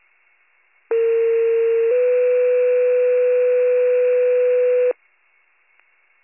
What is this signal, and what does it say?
Two-tone page, heard as a one-second tone, then a three-second tone, about 460 then 500 Hz. It is heard from 0.9 to 4.9 s.